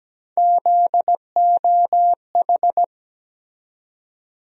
Morse code, heard as a keyed tone, about 17 wpm, 700 Hz. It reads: ZOH
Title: USA Call Sign Postfixes - 3 Letters 17wpm